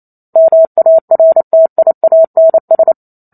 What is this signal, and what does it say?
Morse code, heard as a keyed tone, about 28 wpm, 650 Hz. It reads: MARTIANS